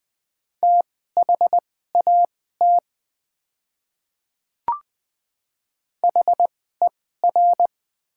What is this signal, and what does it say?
Morse code, heard as a keyed tone, about 20 wpm, spaced as for 14 wpm, 700 Hz. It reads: THAT HER
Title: 2x Character Spacing - Top 100 Words - 20wpm